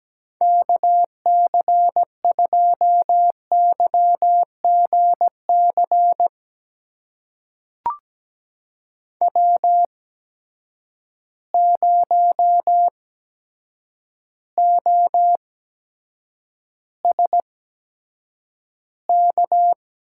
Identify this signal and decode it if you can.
Morse code, heard as a keyed tone, about 17 wpm, spaced as for 8 wpm, 700 Hz. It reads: KC2YGC W0OSK
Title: USA Call Signs - 8x Character Spacing - 17wpm